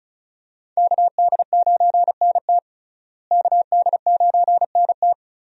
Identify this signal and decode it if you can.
Morse code, heard as a keyed tone, about 34 wpm, 700 Hz. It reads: KD9NT KD9NT
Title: USA Call Signs - 5 Characters - Repeated 2x - 35wpm